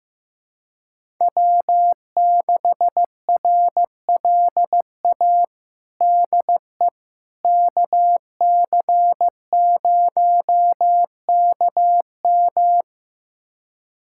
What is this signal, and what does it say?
Morse code, heard as a keyed tone, about 15 wpm, 700 Hz. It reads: W6RLA DE KC0KM
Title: USA Callsign DE Callsign - 5 Characters - 15wpm